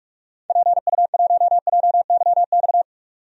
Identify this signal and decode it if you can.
Morse code, heard as a keyed tone, about 42 wpm, 700 Hz. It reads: PU1JYX
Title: DXCC-100 Call Signs - 6 Characters - 45wpm